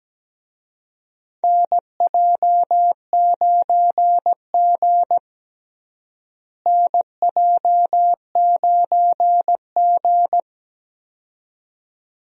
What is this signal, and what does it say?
Morse code, heard as a keyed tone, about 17 wpm, 700 Hz. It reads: NJ9G NJ9G
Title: USA Call Signs - 4 Characters - Repeated 2x - 17wpm